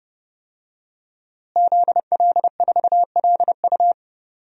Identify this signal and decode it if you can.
Morse code, heard as a keyed tone, about 30 wpm, 700 Hz. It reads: ZL4LU
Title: DXCC-100 Call Signs - 5 Characters - Repeated 3x - 30wpm